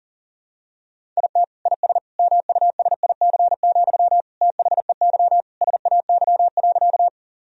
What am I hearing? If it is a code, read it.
Morse code, heard as a keyed tone, about 40 wpm, 700 Hz. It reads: IT IS MUSIC, THEY SAY.
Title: Sentences from Top 1000 Words - Review Part 1 - 40wpm